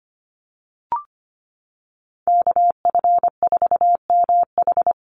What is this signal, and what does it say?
Morse code, heard as a keyed tone, about 25 wpm, 700 Hz. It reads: KF4MH
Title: NA SOTA Chaser Callsigns 25wpm